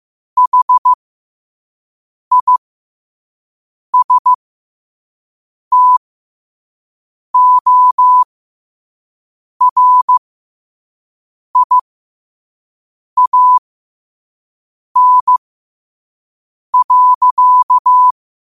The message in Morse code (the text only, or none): HISTORIAN.